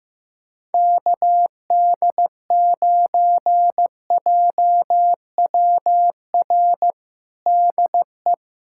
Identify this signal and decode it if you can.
Morse code, heard as a keyed tone, about 15 wpm, 700 Hz. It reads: KD9JWR DE
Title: USA Callsign DE Callsign 15wpm